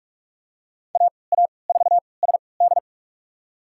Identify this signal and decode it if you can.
Morse code, heard as a keyed tone, about 46 wpm, spaced as for 25 wpm, 700 Hz. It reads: AA4SD
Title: USA Call Signs - 5 Characters - 3x Character Spacing - 45wpm